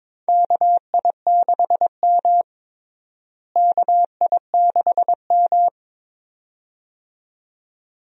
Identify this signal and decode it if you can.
Morse code, heard as a keyed tone, about 22 wpm, 700 Hz. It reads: KI6M KI6M
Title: USA Call Signs - 4 Characters - Repeated 2x - 22wpm